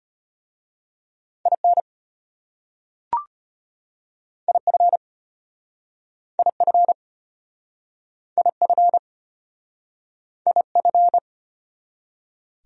Morse code, heard as a keyed tone, about 33 wpm, 700 Hz. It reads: IN IF IF IF IF